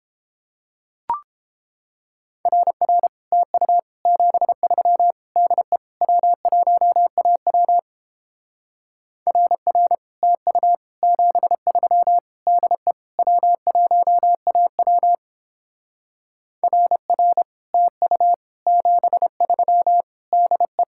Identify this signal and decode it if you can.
Morse code, heard as a keyed tone, about 31 wpm, 700 Hz. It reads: RR TU 73 DE W1AW RR TU 73 DE W1AW RR TU 73 DE